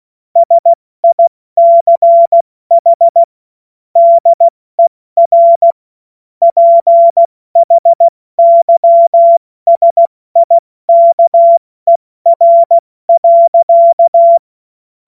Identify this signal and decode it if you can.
Morse code, heard as a keyed tone, about 16 wpm, spaced as for 14 wpm, 680 Hz. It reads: SICH DER PHYSIKER.
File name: Morsekurs / Witze / Witz 21. Dezember 2021, 70 F80